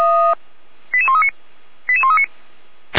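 Five-tone selective call (ZVEI), heard as quick runs of five tones, tones of about 80 ms. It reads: 88139 88139
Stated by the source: ZVEI five-tone selective call received by SDR, recorded by the fire brigade Feuerwehr Orlamünde (audio.f85360.feuerwehr-orlamuende.org).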